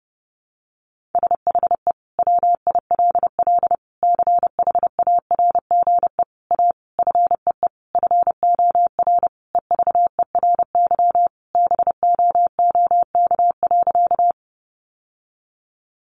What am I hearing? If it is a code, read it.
Morse code, heard as a keyed tone, about 30 wpm, 700 Hz. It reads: SHE WILL CHARGE A FEE FOR EVERY BOOK.
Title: Sentences from Top 1500 Words 30wpm